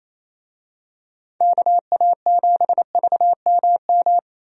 Morse code, heard as a keyed tone, about 27 wpm, 700 Hz. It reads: KA7VMM
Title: USA Call Signs - 6 Characters - 28wpm